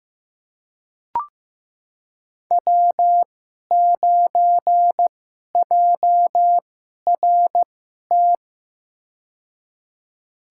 Morse code, heard as a keyed tone, about 15 wpm, spaced as for 11 wpm, 700 Hz. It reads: W9JRT
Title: USA Call Signs - 5 Characters - 2x Character Spacing - 15wpm